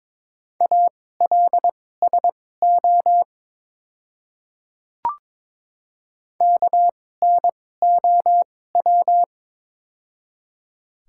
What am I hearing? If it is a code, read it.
Morse code, heard as a keyed tone, about 22 wpm, spaced as for 16 wpm, 700 Hz. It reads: ALSO KNOW
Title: 2x Character Spacing - Top 200 Words - 22wpm